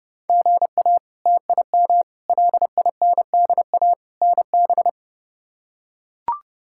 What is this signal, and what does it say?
Morse code, heard as a keyed tone, about 30 wpm, 700 Hz. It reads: GA TIM LINDA NB
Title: K1USN SST QSO Practice - 30wpm